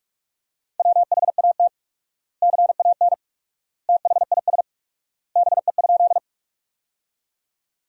Morse code, heard as a keyed tone, about 42 wpm, 700 Hz. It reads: WHAT CAN THIS BE?
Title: 4x Word-Spacing - Sentences from Top 100 Words - 45wpm